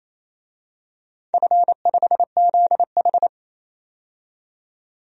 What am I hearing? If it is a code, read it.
Morse code, heard as a keyed tone, about 28 wpm, 690 Hz. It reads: F5ZH
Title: DXCC-100 Call Signs - 4 Characters - 28wpm